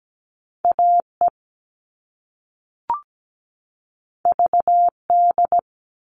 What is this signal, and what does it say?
Morse code, heard as a keyed tone, about 17 wpm, 700 Hz. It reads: AE VD
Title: All Cut Numbers - 2-Digit 17wpm